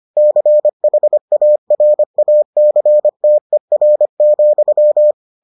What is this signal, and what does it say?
Morse code, heard as a keyed tone, about 25 wpm, 600 Hz. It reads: CHARACTER,